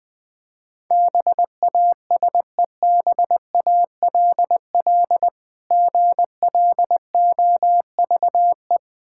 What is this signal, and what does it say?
Morse code, heard as a keyed tone, about 20 wpm, 700 Hz. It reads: BASEBALL GLOVE
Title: Difficult Words with Hints - 20wpm